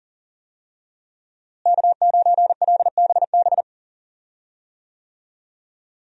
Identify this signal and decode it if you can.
Morse code, heard as a keyed tone, about 38 wpm, 700 Hz. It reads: K9LBB
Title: USA Call Signs - 5 Characters - Repeated 2x - 40wpm